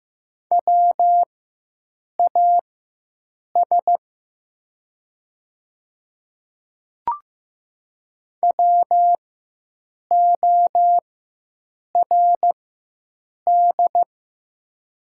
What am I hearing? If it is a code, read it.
Morse code, heard as a keyed tone, about 15 wpm, spaced as for 7 wpm, 700 Hz. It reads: WAS WORD
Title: 4x Character Spacing - Top 100 Words - 15wpm